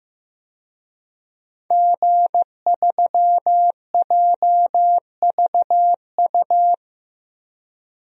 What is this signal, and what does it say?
Morse code, heard as a keyed tone, about 16 wpm, 700 Hz. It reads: G3JVU
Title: DXCC-100 Call Signs - 5 Characters - Repeated 2x - 15wpm